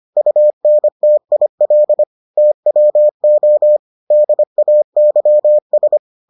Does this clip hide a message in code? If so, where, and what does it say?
Yes: Morse code from 0.2 to 6.0 s, heard as a keyed tone, about 25 wpm, 600 Hz. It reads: UNTIL TWO DAYS